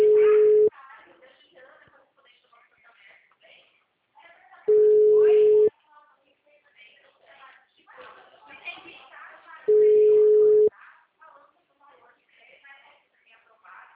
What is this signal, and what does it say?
Ringback tone (Europe, 425 Hz), with speech.
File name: out- -2006- - - .2688